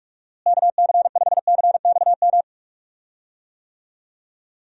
Morse code, heard as a keyed tone, about 42 wpm, 700 Hz. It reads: KC5CXM
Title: USA Call Signs - 6 Characters - Repeated 3x - 45wpm